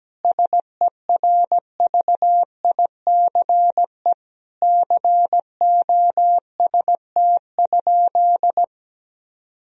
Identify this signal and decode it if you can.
Morse code, heard as a keyed tone, about 17 wpm, 700 Hz. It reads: SERVICE COST?
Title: Sentences from Top 1000 Words - Review Part 4 - 17wpm